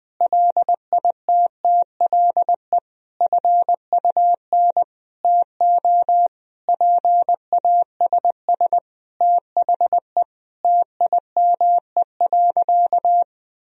Morse code, heard as a keyed tone, about 21 wpm, 700 Hz. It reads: LITTLE FUN TO PASS THE TIME.
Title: Sentences from Top 1000 Words - Review Part 4 - 20wpm